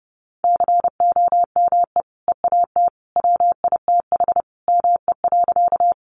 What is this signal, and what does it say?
Morse code, heard as a keyed tone, about 30 wpm, 700 Hz. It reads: COME EAT WITH ME.
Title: Sentences from Top 300 Words 30wpm